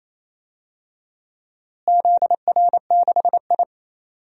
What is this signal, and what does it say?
Morse code, heard as a keyed tone, about 28 wpm, 700 Hz. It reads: ZR6I
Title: DXCC-100 Call Signs - 4 Characters - Repeated 3x - 28wpm